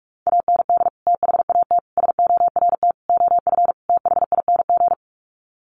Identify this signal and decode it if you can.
Morse code, heard as a keyed tone, about 44 wpm, 690 Hz. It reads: AND THAT SORT OF THING
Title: Sets of 5 Words Encore2 45wpm